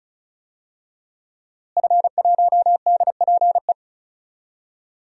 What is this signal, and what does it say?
Morse code, heard as a keyed tone, about 36 wpm, 700 Hz. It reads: F1DPE
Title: DXCC-100 Call Signs - 5 Characters - Repeated 3x - 35wpm